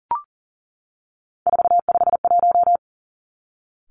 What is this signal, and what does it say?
Morse code, heard as a keyed tone, about 38 wpm, 700 Hz. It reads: 451
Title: Three Challenging Letter-Number Combinations 40wpm